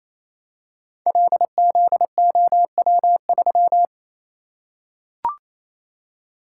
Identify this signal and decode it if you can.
Morse code, heard as a keyed tone, about 27 wpm, 700 Hz. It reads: LZOW3
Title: Code Groups - 5 Letter:Numbers - 28wpm